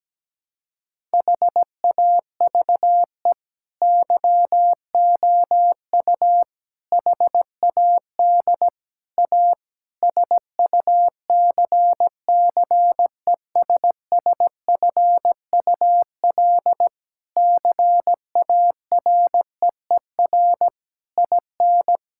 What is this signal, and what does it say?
Morse code, heard as a keyed tone, about 17 wpm, 700 Hz. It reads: HAVE YOU HAD A SUCCESSFUL CAREER IN
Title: Sentences from Top 1000 Words - Review Part 6 - 17wpm